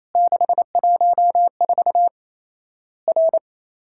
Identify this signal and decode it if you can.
Morse code, heard as a keyed tone, about 28 wpm, 700 Hz. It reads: 614 R
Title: Worked All Europe DX Contest - QTC Exchange - 28wpm